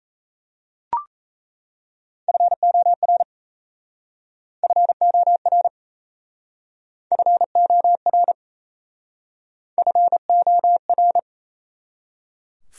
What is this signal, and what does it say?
Morse code, heard as a keyed tone, about 35 wpm, 700 Hz. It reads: FOR FOR FOR FOR
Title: Speed-Racing Top 100 Words 28wpm